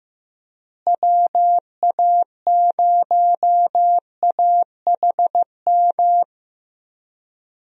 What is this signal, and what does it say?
Morse code, heard as a keyed tone, about 15 wpm, 700 Hz. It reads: WA0AHM
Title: USA Call Signs - 6 Characters - Repeated 3x - 15wpm